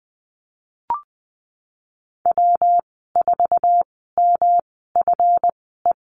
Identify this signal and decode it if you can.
Morse code, heard as a keyed tone, about 20 wpm, spaced as for 15 wpm, 700 Hz. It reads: W4MFE
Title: 140 Lesson 23g - Letter f - Call Signs 20wpm